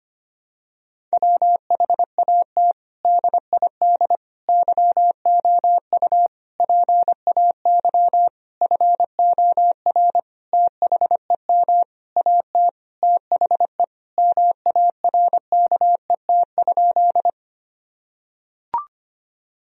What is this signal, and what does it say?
Morse code, heard as a keyed tone, about 25 wpm, 700 Hz. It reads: WHAT DID YOU PAY FOR THEM AT THE MARKET?
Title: Sentences from Top 1000 Words - Review Part 3 - 25wpm